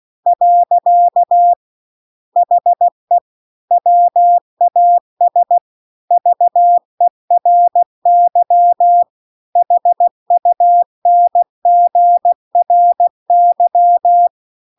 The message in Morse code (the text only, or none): . HE WAS VERY HUNGRY